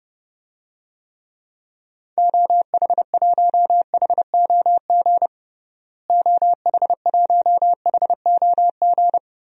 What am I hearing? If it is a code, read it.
Morse code, heard as a keyed tone, about 30 wpm, 700 Hz. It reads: OH1HOG OH1HOG
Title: DXCC-100 Call Signs - 6 Characters - Repeated 2x - 30wpm